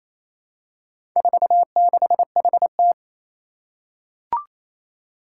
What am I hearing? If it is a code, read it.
Morse code, heard as a keyed tone, about 29 wpm, 700 Hz. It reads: 46HT
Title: Four Challenging Letter-Number Combinations 28wpm